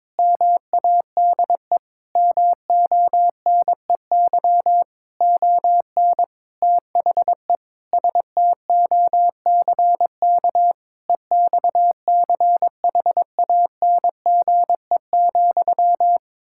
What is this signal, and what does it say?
Morse code, heard as a keyed tone, about 22 wpm, 700 Hz. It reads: MADE MONEY ON THE STOCK EXCHANGE,
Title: Sentences from Top 1000 Words - Review Part 6 - 22wpm